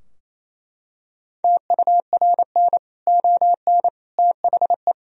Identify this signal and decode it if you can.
Morse code, heard as a keyed tone, about 28 wpm, 700 Hz. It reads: TURN ON THE
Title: Sentences from Top 1000 Words - Review Part 3 - 28wpm